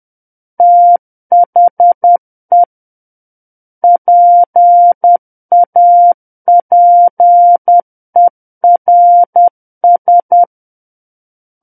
Morse code, heard as a keyed tone, about 10 wpm, 700 Hz. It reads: THE PAPERS
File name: Book One / 01-Chapter 1 - 10 WPM